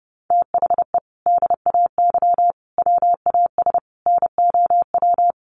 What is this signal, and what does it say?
Morse code, heard as a keyed tone, about 30 wpm, 700 Hz. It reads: THE DAY WAS NOW
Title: Sentences from Top 1200 Words 30wpm